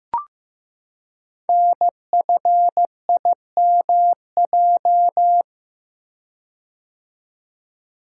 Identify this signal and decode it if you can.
Morse code, heard as a keyed tone, about 15 wpm, 700 Hz. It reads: NFIMJ
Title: Code Groups - 5 Letters - 15wpm